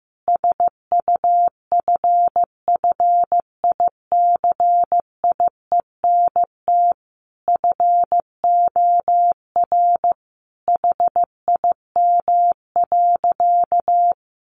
Morse code, heard as a keyed tone, about 15 wpm, 700 Hz. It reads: SUFFICIENT FOR HIM.
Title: Sentences from Top 1400 Words 15wpm